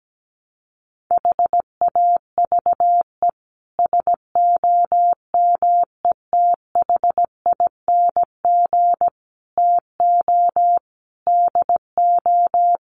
Morse code, heard as a keyed tone, about 17 wpm, 700 Hz. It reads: HAVE SOMETHING TO DO
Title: Sets of 5 Words Encore 17wpm